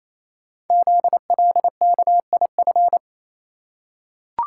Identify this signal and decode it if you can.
Morse code, heard as a keyed tone, about 27 wpm, 700 Hz. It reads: ZLKIF
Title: Code Groups - 5 Letters - 28wpm